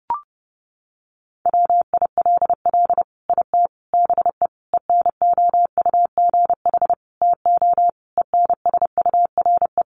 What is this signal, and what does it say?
Morse code, heard as a keyed tone, about 30 wpm, 700 Hz. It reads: WILL IT BE ENOUGH TO ENSURE
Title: Sentences from Top 1100 Words 30wpm